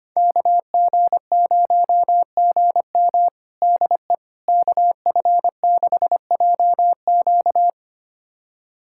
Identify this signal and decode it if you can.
Morse code, heard as a keyed tone, about 25 wpm, 700 Hz. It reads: KG0GM DE KF6JQ
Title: USA Callsign DE Callsign - 5 Characters - 25wpm